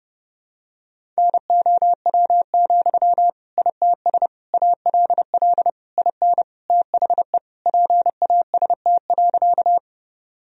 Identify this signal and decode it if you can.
Morse code, heard as a keyed tone, about 30 wpm, 700 Hz. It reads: NOW, ITS ALL IN THE PAST.
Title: Sentences from Top 1000 Words - Review Part 3 - 30wpm